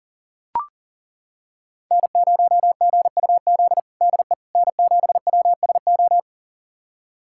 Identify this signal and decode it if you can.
Morse code, heard as a keyed tone, about 38 wpm, 700 Hz. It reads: N0GUZ DE N7WSO
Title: USA Callsign DE Callsign - 5 Characters - 40wpm